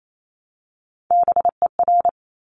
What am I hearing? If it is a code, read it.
Morse code, heard as a keyed tone, about 28 wpm, 700 Hz. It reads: BER
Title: Top Three Letter Combinations in Words 28wpm